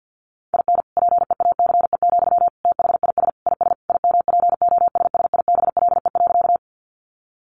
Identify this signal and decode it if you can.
Morse code, heard as a keyed tone, about 50 wpm, 700 Hz. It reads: IN PEACE, THIS IS IMPOSSIBLE.
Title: Sentences from Top 1500 Words 50wpm